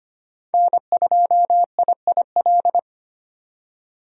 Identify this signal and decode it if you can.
Morse code, heard as a keyed tone, about 25 wpm, 700 Hz. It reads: N2IIL
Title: USA Call Signs - 5 Characters - 25wpm